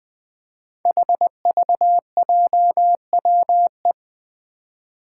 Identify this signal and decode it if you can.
Morse code, heard as a keyed tone, about 20 wpm, 700 Hz. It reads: HVJWE
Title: Five Challenging Letter-Number Combinations 20wpm